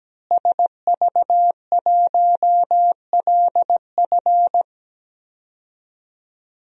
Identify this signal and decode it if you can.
Morse code, heard as a keyed tone, about 17 wpm, 700 Hz. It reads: SV1LF
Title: DXCC-100 Call Signs - 5 Characters - Repeated 3x - 17wpm